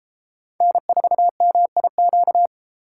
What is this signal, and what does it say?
Morse code, heard as a keyed tone, about 31 wpm, 700 Hz. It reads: N4MIQ